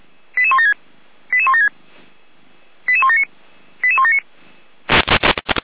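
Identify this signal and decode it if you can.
Five-tone selective call (ZVEI), heard as quick runs of five tones, tones of about 70 ms. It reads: 88176 88176 88179 88179
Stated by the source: ZVEI five-tone selective call received with an SDR.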